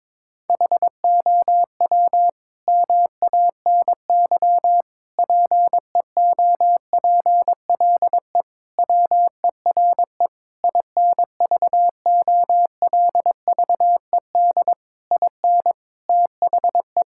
Morse code, heard as a keyed tone, about 22 wpm, 700 Hz. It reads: HOW MANY PEOPLE WERE INVOLVED IN THE
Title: Sentences from Top 1000 Words - Review Part 5 - 22wpm